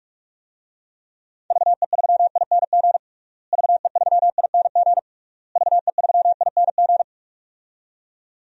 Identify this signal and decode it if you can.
Morse code, heard as a keyed tone, about 41 wpm, 700 Hz. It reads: VE3ING VE3ING VE3ING
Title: DXCC-100 Call Signs - 6 Characters - Repeated 3x - 45wpm